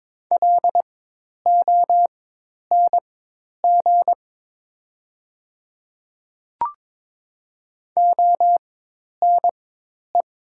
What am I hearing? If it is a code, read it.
Morse code, heard as a keyed tone, about 22 wpm, spaced as for 10 wpm, 700 Hz. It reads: LONG ONE
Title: 4x Character Spacing - Top 100 Words - 22wpm